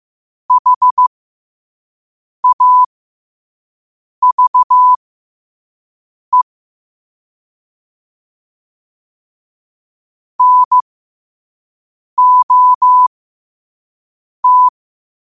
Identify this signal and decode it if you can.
Morse code, heard as a keyed tone, about 15 wpm, spaced as for 5 wpm, 1000 Hz. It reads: HAVE NOT
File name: qotdfiles / 5wpm